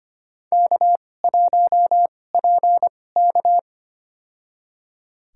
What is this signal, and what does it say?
Morse code, heard as a keyed tone, about 25 wpm, spaced as for 18 wpm, 700 Hz. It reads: K1PK
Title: USA Call Signs - 4 Characters - 2x Character Spacing - 25wpm